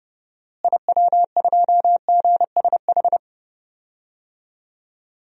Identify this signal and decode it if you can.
Morse code, heard as a keyed tone, about 30 wpm, 700 Hz. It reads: IW2GSH
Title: DXCC-100 Call Signs - 6 Characters - Repeated 3x - 30wpm